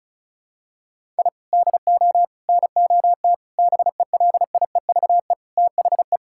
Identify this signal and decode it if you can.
Morse code, heard as a keyed tone, about 36 wpm, 700 Hz. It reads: I DO NOT BELIEVE THE
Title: Sentences from Top 1000 Words - Review Part 6 - 35wpm